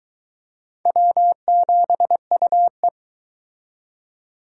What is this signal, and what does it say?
Morse code, heard as a keyed tone, about 22 wpm, 700 Hz. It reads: W7UE